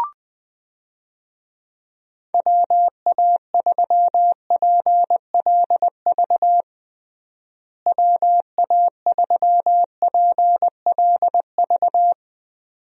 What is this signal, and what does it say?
Morse code, heard as a keyed tone, about 20 wpm, 700 Hz. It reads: WA3PLV WA3PLV
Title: USA Call Signs - 6 Characters - Repeated 2x - 20wpm